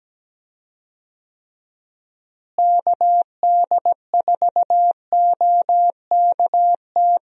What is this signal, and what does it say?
Morse code, heard as a keyed tone, about 17 wpm, 700 Hz. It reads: KD4OKT
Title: USA Call Signs - 6 Characters - Repeated 3x - 17wpm